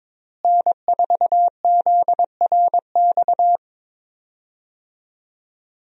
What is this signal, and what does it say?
Morse code, heard as a keyed tone, about 22 wpm, 700 Hz. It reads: N4ZRX
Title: USA Call Signs - 5 Characters - Repeated 3x - 22wpm